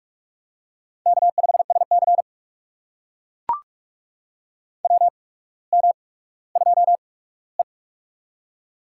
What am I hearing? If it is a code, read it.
Morse code, heard as a keyed tone, about 44 wpm, spaced as for 19 wpm, 700 Hz. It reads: K5SC WM2E
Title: USA Call Signs - 4 Characters - 8x Character Spacing - 45wpm